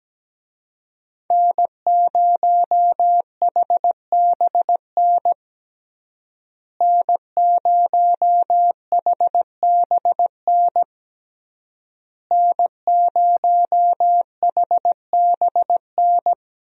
Morse code, heard as a keyed tone, about 17 wpm, 700 Hz. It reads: N0HBN N0HBN N0HBN
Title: USA Call Signs - 5 Characters - Repeated 3x - 17wpm